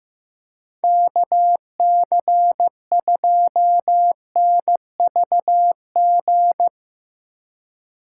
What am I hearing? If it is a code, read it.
Morse code, heard as a keyed tone, about 15 wpm, 700 Hz. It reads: KC2NVG
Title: USA Call Signs - 6 Characters - 4x Character Spacing - 15wpm